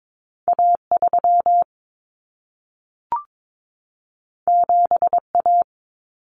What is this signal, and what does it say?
Morse code, heard as a keyed tone, about 22 wpm, 700 Hz. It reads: A3 7A
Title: Common Cut Numbers - 2-Digit 22wpm